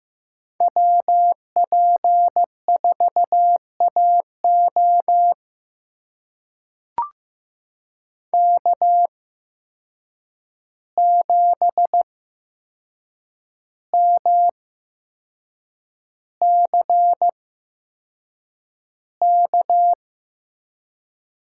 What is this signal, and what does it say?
Morse code, heard as a keyed tone, about 15 wpm, spaced as for 6 wpm, 700 Hz. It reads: WP4AO K7MCK
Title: USA Call Signs - 5 Characters - 8x Character Spacing - 15wpm